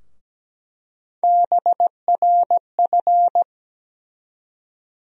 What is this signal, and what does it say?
Morse code, heard as a keyed tone, about 17 wpm, 700 Hz. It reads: BRF
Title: USA Call Sign Postfixes - 3 Letters 17wpm